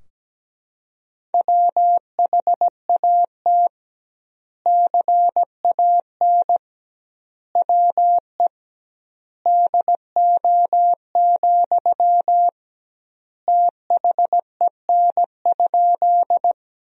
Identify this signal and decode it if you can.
Morse code, heard as a keyed tone, about 17 wpm, 700 Hz. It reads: WHAT CAN WE DO, THEN?
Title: 2x Word-Spacing - Sentences from Top 100 Words - 17wpm